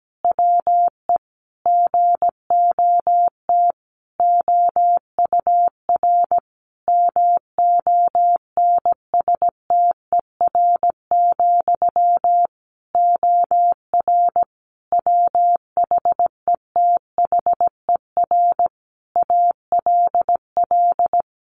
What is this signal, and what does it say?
Morse code, heard as a keyed tone, about 17 wpm, 700 Hz. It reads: WE GOT OUR MONSTER, OR WHETHER ALL
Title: A Sound Of Thunder 17wpm